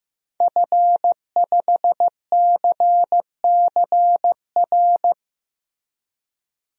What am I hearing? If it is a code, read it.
Morse code, heard as a keyed tone, about 15 wpm, 700 Hz. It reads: F5CCR
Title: DXCC-100 Call Signs - 5 Characters - Repeated 3x - 15wpm